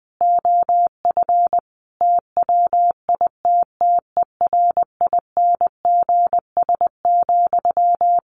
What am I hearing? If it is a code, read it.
Morse code, heard as a keyed tone, about 20 wpm, 700 Hz. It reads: OF TWITTERINGS,
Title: A Sound Of Thunder 20wpm